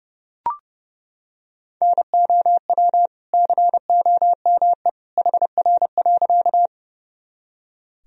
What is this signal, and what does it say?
Morse code, heard as a keyed tone, about 30 wpm, 700 Hz. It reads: NOW COME HR.
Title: Sentences from Top 200 Words and Abbreviations 30wpm